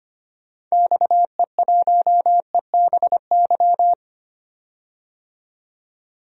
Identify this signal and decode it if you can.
Morse code, heard as a keyed tone, about 25 wpm, 700 Hz. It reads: XE1EBY
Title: DXCC-100 Call Signs - 6 Characters - 25wpm